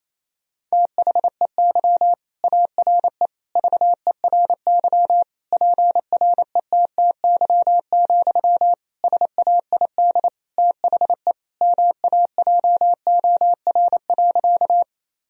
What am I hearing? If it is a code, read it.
Morse code, heard as a keyed tone, about 27 wpm, 700 Hz. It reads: THEY ARE VERY PRETTY, SAID THE MAJOR.
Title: Sentences from Top 1000 Words - Review Part 4 - 28wpm